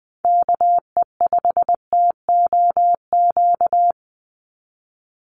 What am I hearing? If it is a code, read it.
Morse code, heard as a keyed tone, about 20 wpm, 700 Hz. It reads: KE5TOQ
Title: USA Call Signs 20wpm